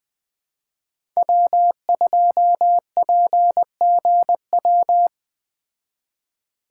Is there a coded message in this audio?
Yes: Morse code from 1.2 to 5.1 s, heard as a keyed tone, about 21 wpm, 700 Hz. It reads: W2PGW